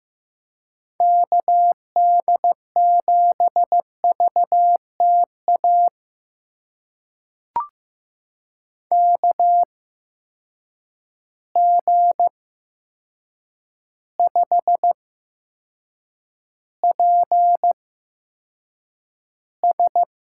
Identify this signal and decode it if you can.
Morse code, heard as a keyed tone, about 15 wpm, spaced as for 7 wpm, 700 Hz. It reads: KD7VTA KG5PS
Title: USA Call Signs - 6 Characters - 8x Character Spacing - 15wpm